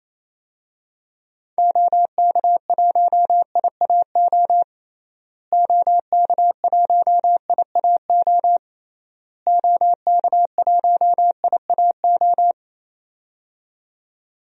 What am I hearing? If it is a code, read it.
Morse code, heard as a keyed tone, about 28 wpm, 700 Hz. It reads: OK1IAO OK1IAO OK1IAO
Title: DXCC-100 Call Signs - 6 Characters - Repeated 3x - 28wpm